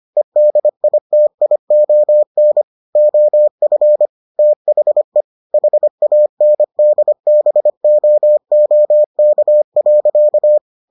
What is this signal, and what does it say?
Morse code, heard as a keyed tone, about 25 wpm, 600 Hz. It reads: EDITION OF THE HANDBOOK.